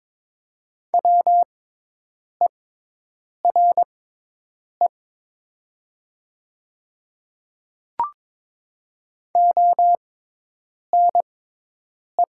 Morse code, heard as a keyed tone, about 22 wpm, spaced as for 8 wpm, 700 Hz. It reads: WERE ONE
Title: 6x Character Spacing - Top 300 Words - 22wpm